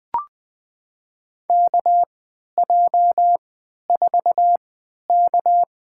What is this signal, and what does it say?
Morse code, heard as a keyed tone, about 21 wpm, spaced as for 12 wpm, 700 Hz. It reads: KJ4K
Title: USA Call Signs - 3x Character Spacing - 20wpm